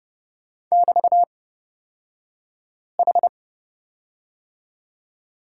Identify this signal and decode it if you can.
Morse code, heard as a keyed tone, about 30 wpm, spaced as for 5 wpm, 700 Hz. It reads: <BT>H
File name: Lesson 36d - BT - Two Characters 30wpm